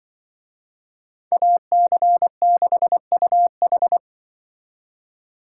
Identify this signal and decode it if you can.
Morse code, heard as a keyed tone, about 25 wpm, 700 Hz. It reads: AC6UH
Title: Speed Racing - USA Call Signs 20wpm